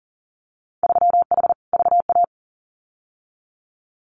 Morse code, heard as a keyed tone, about 40 wpm, 700 Hz. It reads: 3H VA
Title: Winter Field Day Exchanges Encore 40wpm